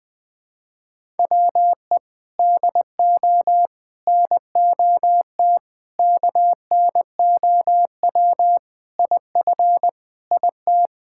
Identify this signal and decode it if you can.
Morse code, heard as a keyed tone, about 20 wpm, 700 Hz. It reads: WE DO NOT KNOW IF IT
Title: Sentences with 4-Word Phrases - 20wpm